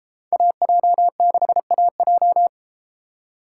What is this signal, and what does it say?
Morse code, heard as a keyed tone, about 32 wpm, 700 Hz. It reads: AJ6AJ